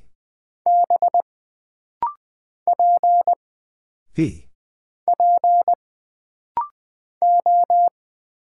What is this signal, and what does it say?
Morse code, heard as a keyed tone, about 20 wpm, 700 Hz. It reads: B P P O
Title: Single Character (Letter:Number:Punctuation:Prosign) - Rapid-Fire - 20wpm